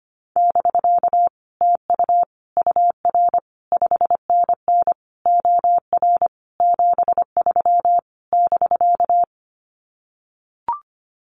Morse code, heard as a keyed tone, about 25 wpm, 700 Hz. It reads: <BK> TU UR 5NN OR 73 <BK>
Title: Common QSO Phrases Encore 25wpm